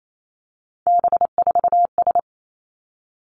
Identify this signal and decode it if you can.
Morse code, heard as a keyed tone, about 28 wpm, 700 Hz. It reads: B4S
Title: Three Challenging Letter-Number Combinations 28wpm